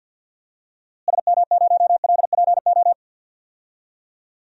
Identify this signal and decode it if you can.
Morse code, heard as a keyed tone, about 48 wpm, 700 Hz. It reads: SM0LPO
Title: DXCC-100 Call Signs - 6 Characters - Repeated 2x - 50wpm